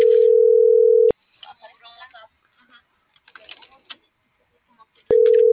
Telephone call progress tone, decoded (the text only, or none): ringback tone